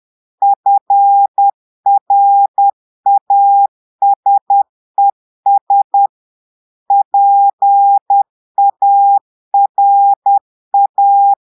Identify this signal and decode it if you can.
Morse code, heard as a keyed tone, about 10 wpm, 800 Hz. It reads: FRASES PARA
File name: portugais texte en clair 2 vitesse 10 mots minutes